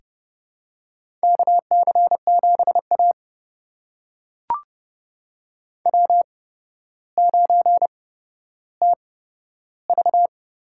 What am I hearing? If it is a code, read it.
Morse code, heard as a keyed tone, about 30 wpm, spaced as for 13 wpm, 700 Hz. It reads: KC7A W9TV
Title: USA Call Signs - 4 Characters - 8x Character Spacing - 30wpm